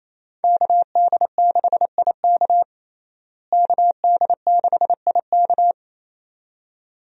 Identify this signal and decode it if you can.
Morse code, heard as a keyed tone, about 28 wpm, 700 Hz. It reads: KD6IK KD6IK
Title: USA Call Signs - 5 Characters - Repeated 2x - 28wpm